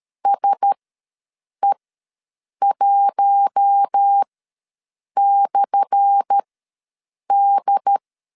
Morse code, heard as a keyed tone, about 13 wpm, spaced as for 7 wpm, 800 Hz. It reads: SE1/D
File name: LEZIONI / 10 SIGNS / 10A SLASH ws 12wpm 3443